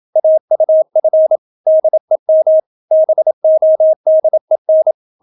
Morse code, heard as a keyed tone, about 27 wpm, 630 Hz. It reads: AUF DEM BODEN